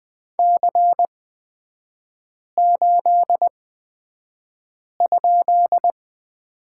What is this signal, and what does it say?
Morse code, heard as a keyed tone, about 20 wpm, spaced as for 5 wpm, 700 Hz. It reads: C8?
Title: 224 Lesson 36e - BT - Three Characters 20wpm